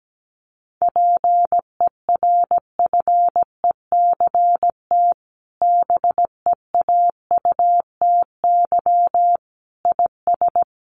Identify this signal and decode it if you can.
Morse code, heard as a keyed tone, about 17 wpm, 700 Hz. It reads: PERFECT BEAUTY IS
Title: Sentences from Top 1200 Words 17wpm